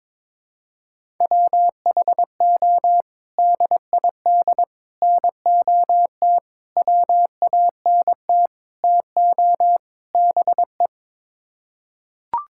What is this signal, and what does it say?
Morse code, heard as a keyed tone, about 22 wpm, 700 Hz. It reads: WHO DID NOT WANT TO BE
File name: practice / 22wpm / Sets of 6 Words - 22wpm